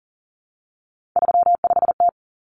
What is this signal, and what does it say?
Morse code, heard as a keyed tone, about 40 wpm, 700 Hz. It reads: 35T
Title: Common Cut Numbers - 3-Digit 40wpm